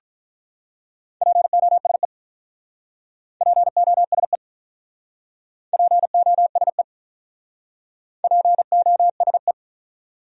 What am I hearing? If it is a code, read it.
Morse code, heard as a keyed tone, about 46 wpm, 700 Hz. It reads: POSE POSE POSE POSE